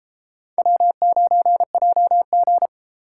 Morse code, heard as a keyed tone, about 32 wpm, 700 Hz. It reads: W9JG